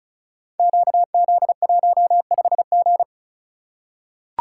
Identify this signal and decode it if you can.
Morse code, heard as a keyed tone, about 36 wpm, 700 Hz. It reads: QZ15G
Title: Code Groups - 5 Letter:Numbers - 35wpm